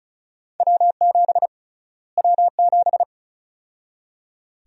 Morse code, heard as a keyed tone, about 34 wpm, 700 Hz. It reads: W7 W7
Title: USA Call Sign Prefixes - Repeated 2x - 35wpm